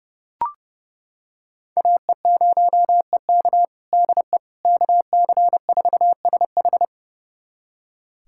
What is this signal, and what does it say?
Morse code, heard as a keyed tone, about 30 wpm, 700 Hz. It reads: AE0EK DE KC4SH
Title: USA Callsign DE Callsign - 5 Characters - 30wpm